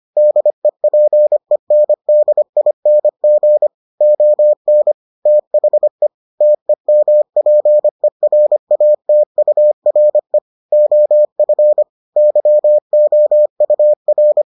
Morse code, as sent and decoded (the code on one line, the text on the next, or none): -.. .--. -. -.. .. -. --. / --- -. / - .... / - . -- .--. .-. .- - ..- .-. / --- ..-. / -.-- --- ..- .-.
DEPENDING ON THE TEMPERATURE OF YOUR